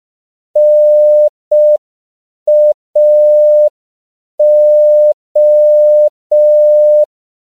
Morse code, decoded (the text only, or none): NAO